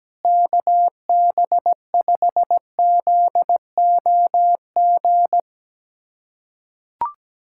Morse code, heard as a keyed tone, about 17 wpm, 700 Hz. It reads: KB5ZOG
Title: USA Callsign DE Callsign - 6 Characters - 17wpm